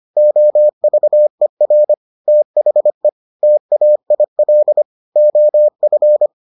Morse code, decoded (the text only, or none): OVER THE TAIL OF